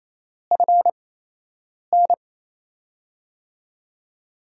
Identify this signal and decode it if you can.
Morse code, heard as a keyed tone, about 28 wpm, spaced as for 8 wpm, 700 Hz. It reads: FN
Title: USA Call Signs - 4 Characters - 8x Character Spacing - 28wpm